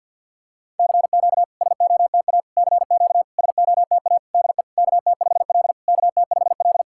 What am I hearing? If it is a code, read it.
Morse code, heard as a keyed tone, about 52 wpm, 690 Hz. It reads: CQ SOTA CQ SOTA DE KT5L KT5L
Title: SOTA QSO Practice 50wpm